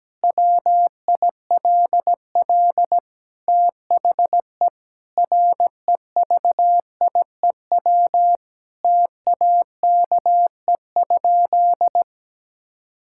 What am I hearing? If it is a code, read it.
Morse code, heard as a keyed tone, about 17 wpm, 700 Hz. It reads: WILL THE REVIEW TAKE?
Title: Sentences from Top 1000 Words - Review Part 6 - 17wpm